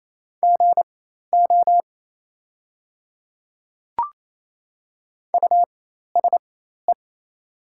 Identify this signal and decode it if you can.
Morse code, heard as a keyed tone, about 28 wpm, spaced as for 13 wpm, 700 Hz. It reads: GO USE
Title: 4x Character Spacing - Top 100 Words - 28wpm